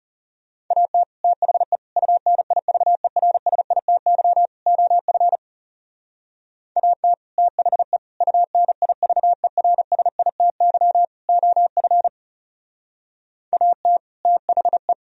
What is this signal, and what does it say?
Morse code, heard as a keyed tone, about 34 wpm, 700 Hz. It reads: AT THE UNIVERSITY OF AT THE UNIVERSITY OF AT THE
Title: Speed-Racing - Sets of 4 Words 30wpm